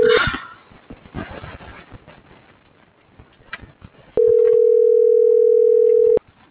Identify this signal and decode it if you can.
Ringback tone (North America), 440 + 480 Hz.